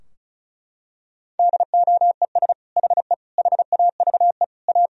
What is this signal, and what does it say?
Morse code, heard as a keyed tone, about 34 wpm, 700 Hz. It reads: DOES HE HAVE A
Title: Sentences from Top 1000 Words - Review Part 2 - 35wpm